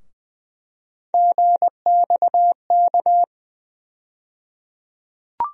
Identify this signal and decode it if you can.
Morse code, heard as a keyed tone, about 20 wpm, 700 Hz. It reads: GXK
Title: USA Call Sign Postfixes - 3 Letters 20wpm